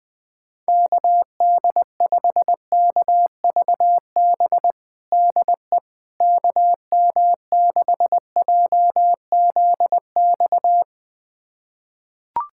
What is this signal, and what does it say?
Morse code, heard as a keyed tone, about 20 wpm, 700 Hz. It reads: KD5KVB DE KM6JZX
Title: USA Callsign DE Callsign - 6 Characters - 20wpm